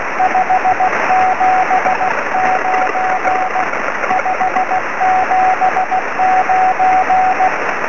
Morse code, heard as a keyed tone, about 16 wpm, 740 Hz, with radio noise. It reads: 579 579
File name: sm7lcb / 20 23cm / 2308 oe5vrl-5 23cm